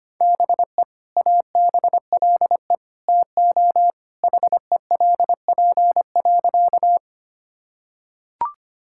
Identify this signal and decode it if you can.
Morse code, heard as a keyed tone, about 25 wpm, 700 Hz. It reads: BE ABLE TO HELP.
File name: practice / 25wpm / Sentences with 2-Word Phrases - 25wpm